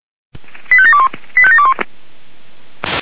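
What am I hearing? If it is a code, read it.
Five-tone selective call (ZVEI), heard as quick runs of five tones, tones of about 70 ms. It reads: 85721 85721
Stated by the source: ZVEI five-tone call received via SDR.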